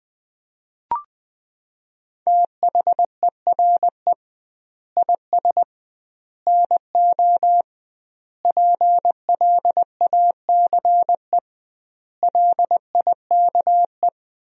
Morse code, heard as a keyed tone, about 20 wpm, 700 Hz. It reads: THERE IS NO PLACE LIKE
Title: 2x Word-Spacing - Sentences from Top 200 Words - 20wpm